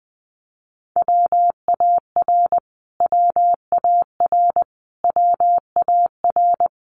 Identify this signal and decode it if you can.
Morse code, heard as a keyed tone, about 19 wpm, 700 Hz. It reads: WAR WAR WAR
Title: Top 500 Words Repeated 20wpm